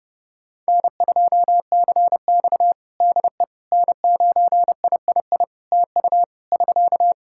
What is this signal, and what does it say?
Morse code, heard as a keyed tone, about 32 wpm, 700 Hz. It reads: N2CX DE N9III TU <SK>
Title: K1USN SST QSO Practice 30wpm